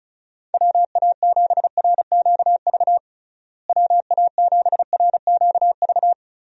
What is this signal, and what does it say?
Morse code, heard as a keyed tone, about 36 wpm, 700 Hz. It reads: WA7RQV WA7RQV
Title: USA Call Signs - 6 Characters - Repeated 3x - 35wpm